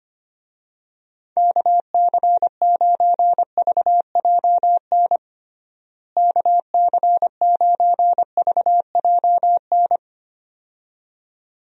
Morse code, heard as a keyed tone, about 25 wpm, 700 Hz. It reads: KC9VJN KC9VJN
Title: USA Call Signs - 6 Characters - Repeated 2x - 25wpm